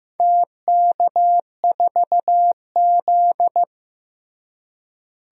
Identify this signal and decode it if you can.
Morse code, heard as a keyed tone, about 15 wpm, 700 Hz. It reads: TK4Z
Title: DXCC-100 Call Signs - 4 Characters - Repeated 2x - 15wpm